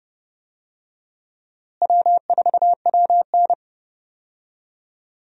Morse code, heard as a keyed tone, about 30 wpm, 700 Hz. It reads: W4WN
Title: USA Call Signs - 4 Characters - Repeated 2x - 30wpm